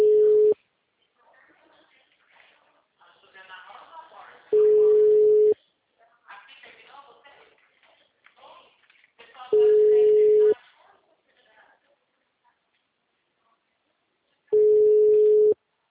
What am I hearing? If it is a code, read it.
Ringback tone (Europe, 425 Hz), with speech.